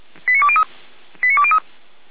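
Five-tone selective call (ZVEI), heard as quick runs of five tones, tones of about 70 ms. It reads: 89203 89203
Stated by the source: ZVEI five-tone selective call received on an SDR.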